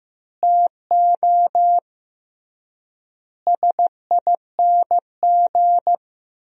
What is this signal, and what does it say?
Morse code, heard as a keyed tone, about 15 wpm, 700 Hz. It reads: TO SING
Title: 3x Word-Spacing - Sentences from Top 400 Words - 15wpm